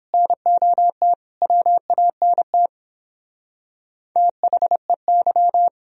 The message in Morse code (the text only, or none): NOT WANT THEY